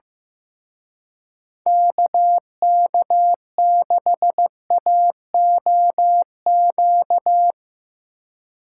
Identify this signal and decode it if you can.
Morse code, heard as a keyed tone, about 16 wpm, 700 Hz. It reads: KK6AOQ
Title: Speed Racing - USA 2x3 Call Signs 15wpm